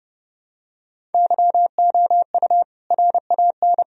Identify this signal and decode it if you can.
Morse code, heard as a keyed tone, about 30 wpm, 700 Hz. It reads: YOU RAN
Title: Sentences from Top 1000 Words - Review Part 3 - 30wpm